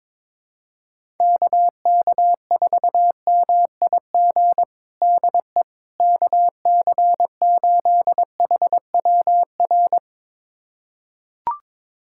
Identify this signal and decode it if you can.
Morse code, heard as a keyed tone, about 22 wpm, 700 Hz. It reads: KK4MIG DE KC8HWR
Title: USA Callsign DE Callsign 22wpm